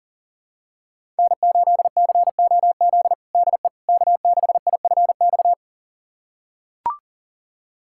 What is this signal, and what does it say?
Morse code, heard as a keyed tone, about 40 wpm, 700 Hz. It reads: N8COZ DE K6IFX
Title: USA Callsign DE Callsign - 5 Characters - 40wpm